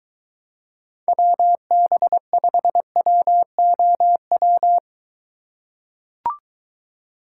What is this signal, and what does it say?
Morse code, heard as a keyed tone, about 22 wpm, 700 Hz. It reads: WB5WOW